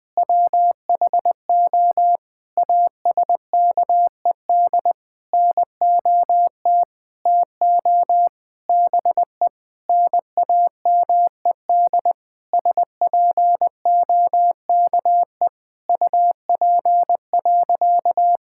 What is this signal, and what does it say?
Morse code, heard as a keyed tone, about 20 wpm, 700 Hz. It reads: WHO ASKED NOT TO BE NAMED SPOKE UP.
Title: Sentences with 6-Word Phrases - Spoken Beginning - 20wpm